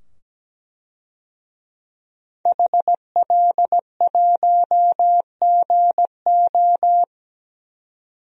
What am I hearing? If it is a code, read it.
Morse code, heard as a keyed tone, about 17 wpm, 700 Hz. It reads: HL1GO
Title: DXCC-100 Call Signs - 5 Characters - 17wpm